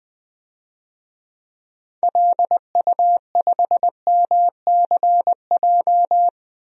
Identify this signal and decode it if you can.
Morse code, heard as a keyed tone, about 20 wpm, 700 Hz. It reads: LU5MCJ
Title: DXCC-100 Call Signs - 6 Characters - 20wpm